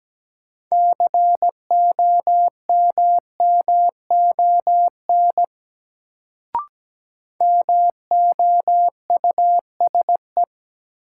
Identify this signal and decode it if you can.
Morse code, heard as a keyed tone, about 17 wpm, 700 Hz. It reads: COMMON MOUSE
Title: Sentences from Top 700 Words - New Word Drill - 17wpm